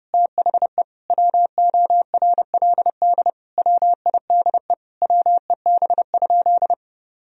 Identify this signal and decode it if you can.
Morse code, heard as a keyed tone, about 32 wpm, 700 Hz. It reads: THE WORLD WIDE WEB?
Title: Sentences from Top 1000 Words - Review Part 4 - 30wpm